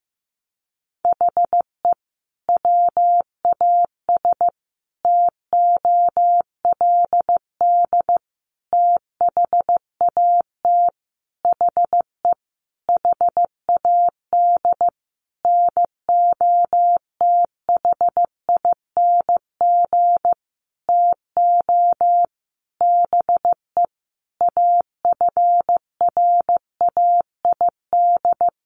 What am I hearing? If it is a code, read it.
Morse code, heard as a keyed tone, about 15 wpm, 700 Hz. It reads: HE WAS TOLD THAT HE HAD NOTHING TO BE AFRAID
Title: Sentences from Top 1500 Words 15wpm